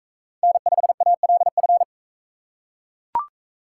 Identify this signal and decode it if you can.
Morse code, heard as a keyed tone, about 42 wpm, 700 Hz. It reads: N5ALF